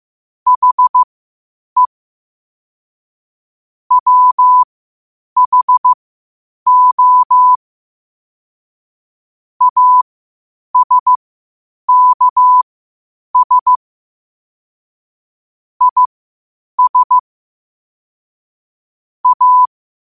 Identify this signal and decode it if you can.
Morse code, heard as a keyed tone, about 15 wpm, spaced as for 8 wpm, 1000 Hz. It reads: HE WHO ASKS IS A
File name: qotdfiles / 8wpm